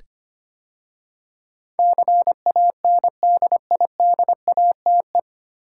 Morse code, heard as a keyed tone, about 25 wpm, 700 Hz. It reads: CANDIDATE
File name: practice / 25wpm / 9 Letter Words - First 3 Letters Given - 25wpm